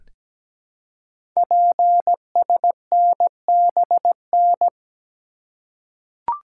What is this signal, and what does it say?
Morse code, heard as a keyed tone, about 17 wpm, 700 Hz. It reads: PSNBN
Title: Code Groups - 5 Letters - 17wpm